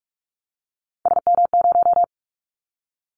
Morse code, heard as a keyed tone, about 46 wpm, 700 Hz. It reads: SM0
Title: Callsign Prefixes for Top 25 DXCC - 45wpm